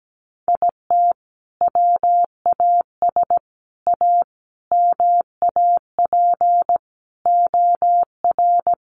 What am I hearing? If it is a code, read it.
Morse code, heard as a keyed tone, about 17 wpm, 700 Hz. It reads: IT WAS A MAP OR
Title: Sentences from Top 1300 Words 17wpm